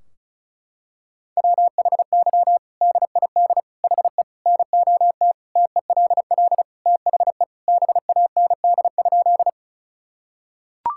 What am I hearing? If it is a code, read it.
Morse code, heard as a keyed tone, about 34 wpm, 700 Hz. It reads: WHY DID HE NOT TELL THE BAND?
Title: Sentences from Top 1600 Words 35wpm